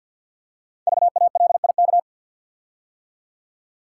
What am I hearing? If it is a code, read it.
Morse code, heard as a keyed tone, about 50 wpm, 700 Hz. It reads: VALIK
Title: Code Groups - 5 Letters - 50wpm